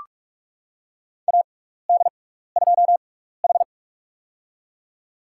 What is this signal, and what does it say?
Morse code, heard as a keyed tone, about 46 wpm, spaced as for 16 wpm, 700 Hz. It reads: AD2H
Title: USA Call Signs - 4 Characters - 6x Character Spacing - 45wpm